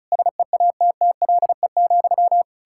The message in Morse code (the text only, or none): SEATTLE,